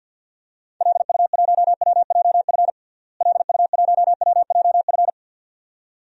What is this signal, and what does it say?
Morse code, heard as a keyed tone, about 52 wpm, 700 Hz. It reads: LU1WJF LU1WJF
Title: DXCC-100 Call Signs - 6 Characters - Repeated 2x - 50wpm